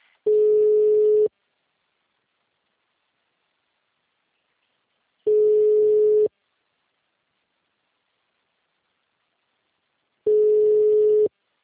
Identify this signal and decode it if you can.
Ringback tone (Europe, 425 Hz).
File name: out- -2002- - - .4